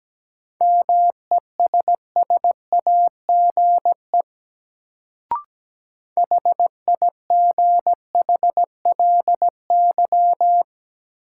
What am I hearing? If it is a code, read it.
Morse code, heard as a keyed tone, about 17 wpm, 700 Hz. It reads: MESSAGE HIGHLY
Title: Sentences from Top 1100 Words - New Word Drill - 17wpm